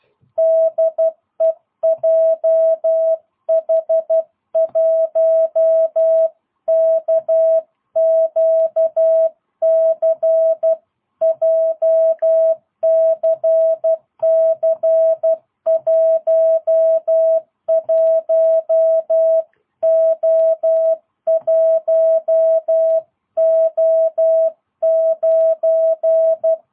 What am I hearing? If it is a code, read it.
Morse code, heard as a keyed tone, about 12 wpm, 660 Hz. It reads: DE JH1KQC JCC11O1O9
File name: jh8cbh / 2022cw